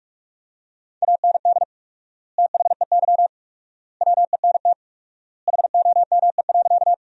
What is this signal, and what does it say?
Morse code, heard as a keyed tone, about 41 wpm, 700 Hz. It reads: AND THEY WENT HOME.
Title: 4x Word-Spacing - Sentences from Top 200 Words - 45wpm